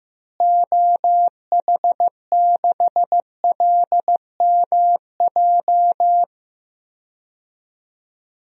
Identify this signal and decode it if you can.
Morse code, heard as a keyed tone, about 15 wpm, 700 Hz. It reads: OH6LMJ
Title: DXCC-100 Call Signs - 6 Characters - Repeated 3x - 15wpm